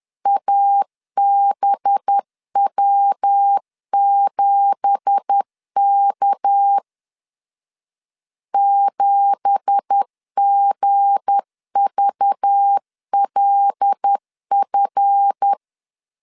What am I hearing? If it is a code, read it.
Morse code, heard as a keyed tone, about 10 wpm, 800 Hz. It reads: ABW7K 7GVLF
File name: LEZIONI / 9 2867 10wpm 2733